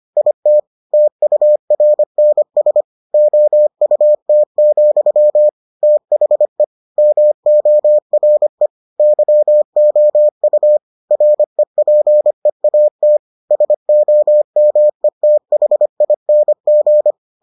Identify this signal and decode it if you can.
Morse code, heard as a keyed tone, about 25 wpm, 600 Hz. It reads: IT TURNS OUT, THE MORE YOU REPEAT SOMETHING